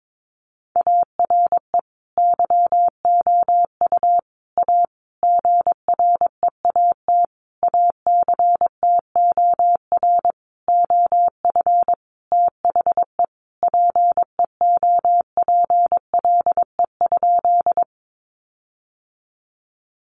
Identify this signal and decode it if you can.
Morse code, heard as a keyed tone, about 22 wpm, 700 Hz. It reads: ARE YOU A GREAT ACTOR OF THE PEOPLE?
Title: Sentences from Top 1500 Words 22wpm